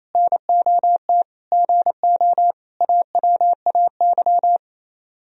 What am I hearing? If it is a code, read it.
Morse code, heard as a keyed tone, about 34 wpm, 700 Hz. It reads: NOT GO AWAY
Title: Speed-Racing - Sets of 4 Words 28wpm